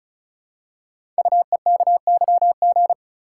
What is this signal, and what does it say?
Morse code, heard as a keyed tone, about 36 wpm, 700 Hz. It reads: UEKYG
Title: Code Groups - 5 Letters - 35wpm